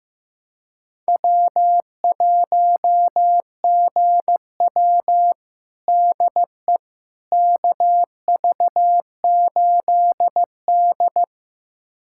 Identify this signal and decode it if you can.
Morse code, heard as a keyed tone, about 15 wpm, 700 Hz. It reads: W1GW DE KV8D
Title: USA Callsign DE Callsign - 4 Characters - 15wpm